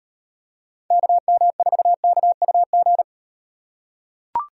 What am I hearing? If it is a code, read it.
Morse code, heard as a keyed tone, about 37 wpm, 700 Hz. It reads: KM4KUG